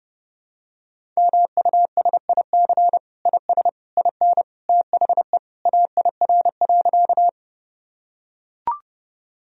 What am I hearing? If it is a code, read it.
Morse code, heard as a keyed tone, about 32 wpm, 700 Hz. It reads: MUSIC IS IN THE AIR.
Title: Sentences from Top 1000 Words - Review Part 1 - 30wpm